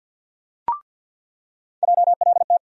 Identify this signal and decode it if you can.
Morse code, heard as a keyed tone, about 53 wpm, 700 Hz. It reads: JLT
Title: USA Call Sign Postfixes - 3 Letters 50wpm